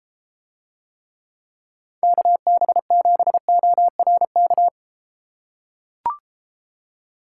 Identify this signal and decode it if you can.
Morse code, heard as a keyed tone, about 32 wpm, 700 Hz. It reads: KB7ORK